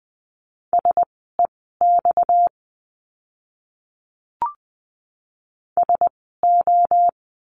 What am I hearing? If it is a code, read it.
Morse code, heard as a keyed tone, about 20 wpm, spaced as for 14 wpm, 700 Hz. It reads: SEX SO
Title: 230 Lesson 37f - Letter x - Words 20wpm